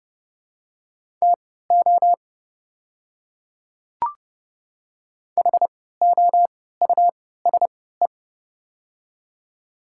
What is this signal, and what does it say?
Morse code, heard as a keyed tone, about 30 wpm, spaced as for 17 wpm, 700 Hz. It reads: TO HOUSE
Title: 3x Character Spacing - Top 300 Words - 30wpm